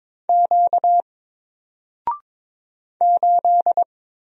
Morse code, heard as a keyed tone, about 22 wpm, 700 Hz. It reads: Q 8
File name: Single-Letter-Number-22wpm